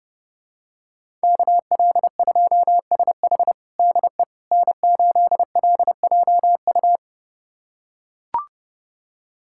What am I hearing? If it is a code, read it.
Morse code, heard as a keyed tone, about 30 wpm, 700 Hz. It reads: KL2SH DE N8LJU
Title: USA Callsign DE Callsign 30wpm